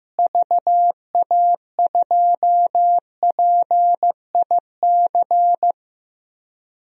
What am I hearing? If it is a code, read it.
Morse code, heard as a keyed tone, about 16 wpm, 700 Hz. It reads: VA2PIC
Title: DXCC-100 Call Signs - 6 Characters - Repeated 3x - 15wpm